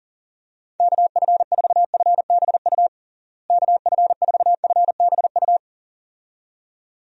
Morse code, heard as a keyed tone, about 40 wpm, 700 Hz. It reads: KF4FBU KF4FBU
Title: USA Call Signs - 6 Characters - Repeated 2x - 40wpm